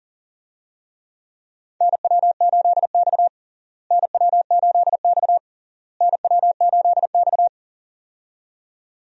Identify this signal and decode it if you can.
Morse code, heard as a keyed tone, about 40 wpm, 700 Hz. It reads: NW8X NW8X NW8X
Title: USA Call Signs - 4 Characters - Repeated 3x - 40wpm